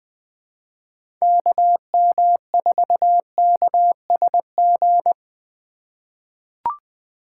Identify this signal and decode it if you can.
Morse code, heard as a keyed tone, about 20 wpm, 700 Hz. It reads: KM4KSG
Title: USA Call Signs - 6 Characters - 2x Character Spacing - 20wpm